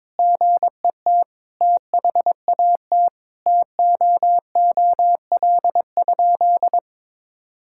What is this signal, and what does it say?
Morse code, heard as a keyed tone, about 22 wpm, 700 Hz. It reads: GET THAT TOOL?
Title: Sentences from Top 1000 Words - Review Part 6 - 22wpm